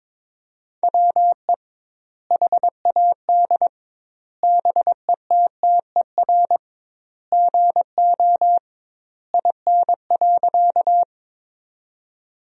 Morse code, heard as a keyed tone, about 22 wpm, 700 Hz. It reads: WE HAD BETTER GO IN.
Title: 2x Word-Spacing - Sentences from Top 400 Words - 22wpm